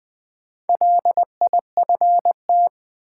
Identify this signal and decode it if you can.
Morse code, heard as a keyed tone, about 21 wpm, 700 Hz. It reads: LIFT
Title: Sentences from Top 1000 Words - New Word Drill - 20wpm